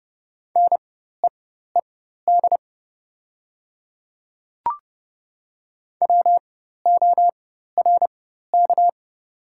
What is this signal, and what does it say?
Morse code, heard as a keyed tone, about 30 wpm, spaced as for 14 wpm, 700 Hz. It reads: NEED WORK
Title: 4x Character Spacing - Top 300 Words - 30wpm